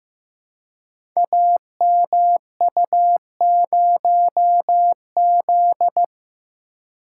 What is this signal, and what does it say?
Morse code, heard as a keyed tone, about 15 wpm, 700 Hz. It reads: AMU0Z
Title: Code Groups - 5 Letter:Numbers - 15wpm